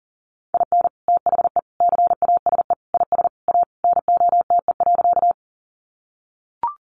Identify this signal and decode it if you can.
Morse code, heard as a keyed tone, about 40 wpm, 700 Hz. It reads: IN THE CASE IS A NOTE.
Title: Sentences from Top 700 Words 40wpm